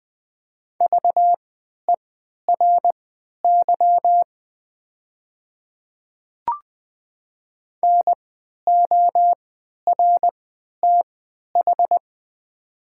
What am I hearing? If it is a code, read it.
Morse code, heard as a keyed tone, about 20 wpm, spaced as for 11 wpm, 700 Hz. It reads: VERY NORTH
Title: 3x Character Spacing - Top 300 Words - 20wpm